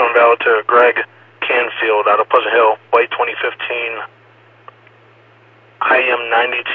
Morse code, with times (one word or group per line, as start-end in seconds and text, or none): none